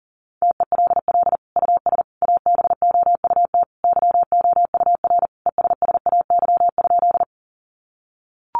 Morse code, heard as a keyed tone, about 40 wpm, 700 Hz. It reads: TELL US ABOUT YOUR ESSAY?
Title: Sentences from Top 1500 Words 40wpm